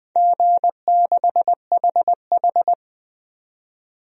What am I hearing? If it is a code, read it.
Morse code, heard as a keyed tone, about 20 wpm, 700 Hz. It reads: G6HH
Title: DXCC-100 Call Signs - 4 Characters - 20wpm